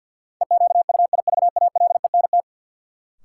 Morse code, heard as a keyed tone, about 49 wpm, 700 Hz. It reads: EQUIVALENT
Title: Sentences from Top 1600 Words - New Word Drill - 50wpm